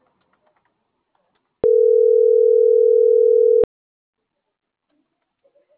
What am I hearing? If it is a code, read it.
Ringback tone (North America), 440 + 480 Hz.